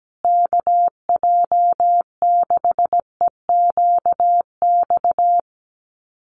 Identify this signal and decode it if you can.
Morse code, heard as a keyed tone, about 17 wpm, 700 Hz. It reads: KJ6EQX
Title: USA Call Signs Repeated 2x 17wpm